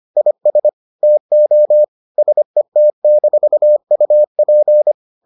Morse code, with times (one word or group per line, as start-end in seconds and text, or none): 0.2-0.7 IS
1.0-1.8 TO
2.2-4.9 SET-UP